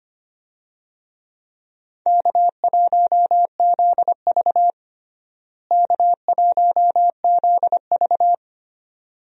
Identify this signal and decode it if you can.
Morse code, heard as a keyed tone, about 25 wpm, 700 Hz. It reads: K1ZV K1ZV
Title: USA Call Signs - 4 Characters - Repeated 2x - 25wpm